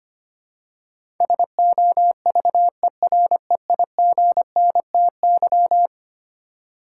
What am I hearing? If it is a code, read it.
Morse code, heard as a keyed tone, about 25 wpm, 700 Hz. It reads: SOVEREIGNTY
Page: morse-code-ninja-practice-25wpm